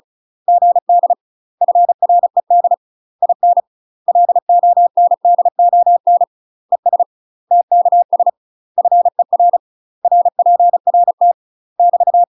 Morse code, heard as a keyed tone, about 35 wpm, 700 Hz. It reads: GD FRED IN LONDON ES TKS FER RPRT <BT>
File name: UK MASTER QSOs TRAINING / 35WPM / UK MASTER QSOs 0041